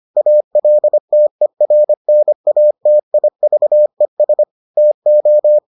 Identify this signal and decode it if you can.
Morse code, heard as a keyed tone, about 25 wpm, 600 Hz. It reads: ALTERNATIVES TO